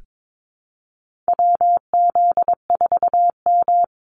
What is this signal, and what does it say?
Morse code, heard as a keyed tone, about 22 wpm, 700 Hz. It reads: WZ4M
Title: NA SOTA Activator Callsigns 22wpm